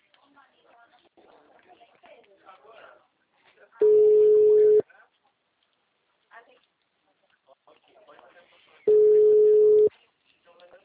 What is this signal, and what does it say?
Ringback tone (Europe, 425 Hz), with speech.
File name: out- -2002- - - .23558